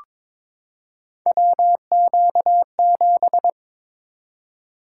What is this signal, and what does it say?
Morse code, heard as a keyed tone, about 22 wpm, 700 Hz. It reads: WQ7
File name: practice / 22wpm / Three Letter-Number Combinations - 22wpm